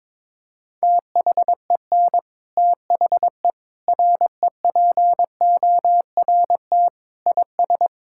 Morse code, heard as a keyed tone, about 21 wpm, 700 Hz. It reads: THEN THE REPORT IS
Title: Sentences from Top 1000 Words - Review Part 6 - 22wpm